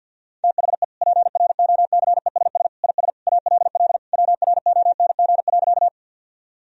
Morse code, heard as a keyed tone, about 50 wpm, 700 Hz. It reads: THE PROCESS IS ALL WRONG.
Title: Sentences from Top 1000 Words - Review Part 3 - 50wpm